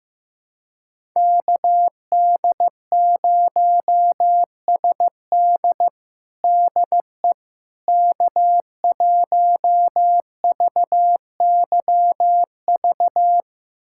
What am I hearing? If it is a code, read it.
Morse code, heard as a keyed tone, about 16 wpm, 700 Hz. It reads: KD0SD DE K1VYV
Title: USA Callsign DE Callsign 15wpm